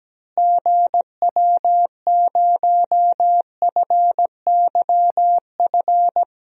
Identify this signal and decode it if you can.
Morse code, heard as a keyed tone, about 17 wpm, 700 Hz. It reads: GW0FYF
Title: DXCC-100 Call Signs - 6 Characters - Repeated 3x - 17wpm